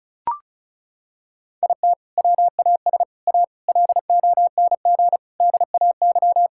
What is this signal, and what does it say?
Morse code, heard as a keyed tone, about 34 wpm, 700 Hz. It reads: IT WAS A LONG DAY
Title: Sentences from Top 1600 Words 35wpm